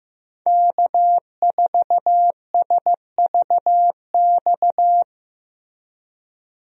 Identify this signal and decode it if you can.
Morse code, heard as a keyed tone, about 15 wpm, 700 Hz. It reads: K4SVX
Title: USA Call Signs - 5 Characters - Repeated 3x - 15wpm